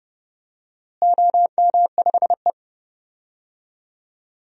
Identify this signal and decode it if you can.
Morse code, heard as a keyed tone, about 30 wpm, 700 Hz. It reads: OM5E
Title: DXCC-100 Call Signs - 4 Characters - Repeated 2x - 30wpm